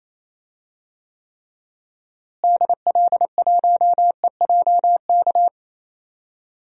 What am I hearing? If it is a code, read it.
Morse code, heard as a keyed tone, about 27 wpm, 700 Hz. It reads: DL1EJK
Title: DXCC-100 Call Signs - 6 Characters - Repeated 3x - 28wpm